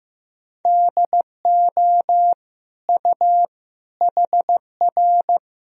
Morse code, heard as a keyed tone, about 16 wpm, 700 Hz. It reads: DO U HR
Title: Sentences from Top 300 Words and Abbreviations 15wpm